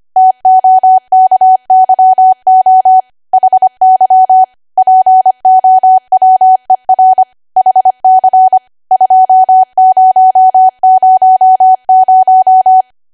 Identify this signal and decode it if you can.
Morse code, heard as a keyed tone, about 25 wpm, 750 Hz. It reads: TOKYO HY POWER HC 2000
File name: Archive / 25 WPM / 25WPM